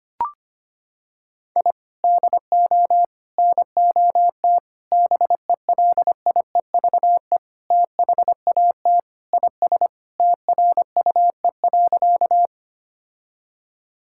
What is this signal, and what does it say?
Morse code, heard as a keyed tone, about 25 wpm, 700 Hz. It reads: I DO NOT BELIEVE THAT IS TRUE.
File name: practice / 25wpm / Sentences with 4-Word Phrases - 25wpm